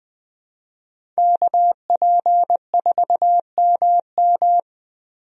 Morse code, heard as a keyed tone, about 20 wpm, 700 Hz. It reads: KP4MM
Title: USA Call Signs - 5 Characters - Repeated 2x - 20wpm